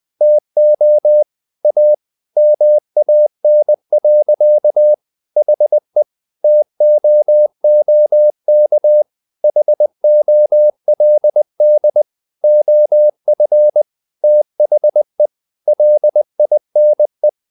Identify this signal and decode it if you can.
Morse code, heard as a keyed tone, about 20 wpm, 600 Hz. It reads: TO A MAN. HE TOOK HOLD OF THE LINE
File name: the old man and the sea part 2 20wpm